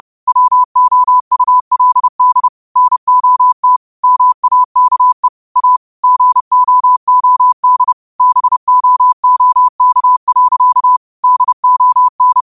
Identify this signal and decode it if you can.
Morse code, heard as a keyed tone, about 30 wpm, 1000 Hz. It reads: WOULD NOT MAKE A GOOD BOOK. DON
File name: qotdfiles / 30wpm